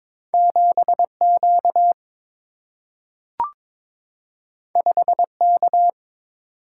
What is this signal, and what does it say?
Morse code, heard as a keyed tone, about 22 wpm, 700 Hz. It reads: 7 Q 5 K
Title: Single-Letter Single-Number 22wpm